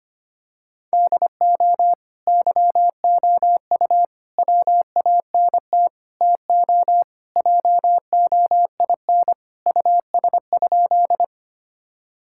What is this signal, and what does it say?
Morse code, heard as a keyed tone, about 25 wpm, 700 Hz. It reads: DO YOU WANT TO JOIN US?
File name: practice / 25wpm / Sentences with 4-Word Phrases - 25wpm